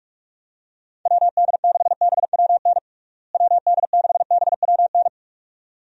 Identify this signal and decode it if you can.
Morse code, heard as a keyed tone, about 46 wpm, 700 Hz. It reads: WD6BWN WD6BWN
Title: USA Call Signs - 6 Characters - Repeated 2x - 45wpm